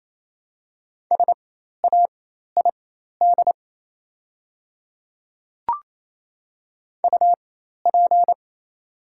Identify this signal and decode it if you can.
Morse code, heard as a keyed tone, about 28 wpm, spaced as for 13 wpm, 700 Hz. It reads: SAID UP
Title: 4x Character Spacing - Top 100 Words - 28wpm